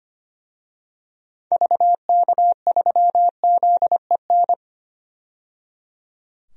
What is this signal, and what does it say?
Morse code, heard as a keyed tone, about 25 wpm, 700 Hz. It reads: VK3ZEN
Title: DXCC-100 Call Signs - 6 Characters - 25wpm